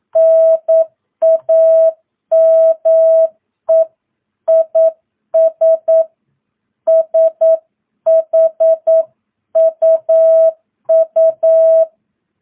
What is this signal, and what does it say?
Morse code, heard as a keyed tone, about 9 wpm, 660 Hz. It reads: NAME IS SHUU
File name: jh8cbh / 2022cw